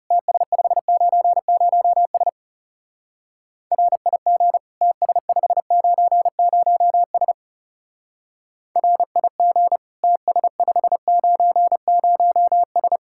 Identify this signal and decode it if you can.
Morse code, heard as a keyed tone, about 36 wpm, 700 Hz. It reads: TS590S RIG TS590S RIG TS590S
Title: Speed-Racing Common QSO Phrases - 30wpm